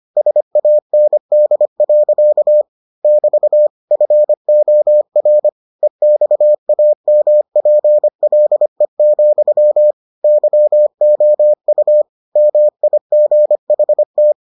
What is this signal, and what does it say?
Morse code, heard as a keyed tone, about 25 wpm, 600 Hz. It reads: SAND. <BT> FOR EXAMPLE, YOU MIGHT